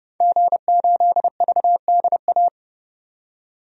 Morse code, heard as a keyed tone, about 30 wpm, 700 Hz. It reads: G8VDA
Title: DXCC-100 Call Signs - 5 Characters - 30wpm